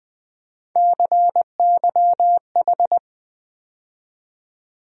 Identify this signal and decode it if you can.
Morse code, heard as a keyed tone, about 20 wpm, 700 Hz. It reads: CYH
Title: USA Call Sign Postfixes - 3 Letters 20wpm